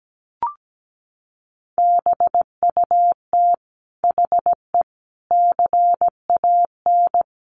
Morse code, heard as a keyed tone, about 17 wpm, 700 Hz. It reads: BUT HE CAN
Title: Sentences from Top 900 Words 17wpm